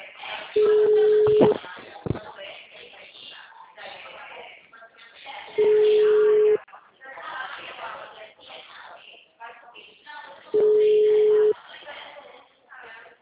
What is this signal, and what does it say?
Ringback tone (Europe, 425 Hz), with speech.
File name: out- -2016